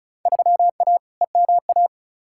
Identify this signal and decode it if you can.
Morse code, heard as a keyed tone, about 34 wpm, 700 Hz. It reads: 3A EMA
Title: Field Day Exchanges - 35wpm